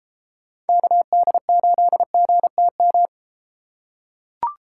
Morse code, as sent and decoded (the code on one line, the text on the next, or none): -.- -.. ---.. --. - --
KD8GTM